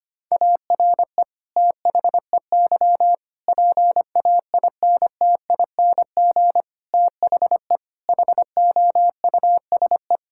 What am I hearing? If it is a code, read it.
Morse code, heard as a keyed tone, about 25 wpm, 700 Hz. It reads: ARE THEY PAINTING THE HOUSE
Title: Sentences from Top 1000 Words - Review Part 6 - 25wpm